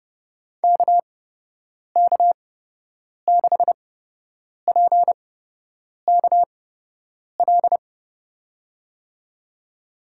Morse code, heard as a keyed tone, about 29 wpm, spaced as for 8 wpm, 700 Hz. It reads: KK6PKL
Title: USA Call Signs - 6 Characters - 8x Character Spacing - 30wpm